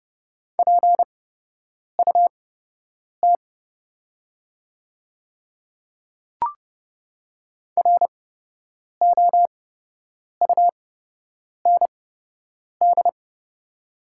Morse code, heard as a keyed tone, about 30 wpm, spaced as for 8 wpm, 690 Hz. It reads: PUT ROUND
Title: 8x Character Spacing - Top 200 Words - 30wpm